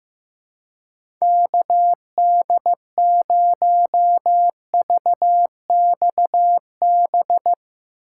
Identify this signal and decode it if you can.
Morse code, heard as a keyed tone, about 15 wpm, 700 Hz. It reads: KD0VXB
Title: USA Call Signs - 6 Characters - 3x Character Spacing - 15wpm